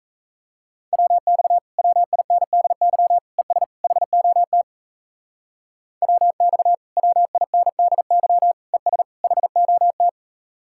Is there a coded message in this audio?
Yes: Morse code from 0.9 to 10.1 s, heard as a keyed tone, about 35 wpm, 700 Hz. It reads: WX WINDY ES HOT WX WINDY ES HOT